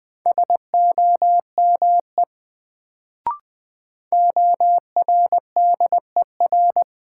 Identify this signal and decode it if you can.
Morse code, heard as a keyed tone, about 20 wpm, 700 Hz. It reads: SOME ORDER
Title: Sets of 2 Words - New Word Drill - 20wpm